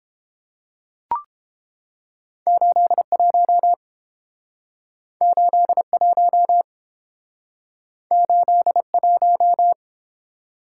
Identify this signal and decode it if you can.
Morse code, heard as a keyed tone, about 30 wpm, 700 Hz. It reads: 81 81 81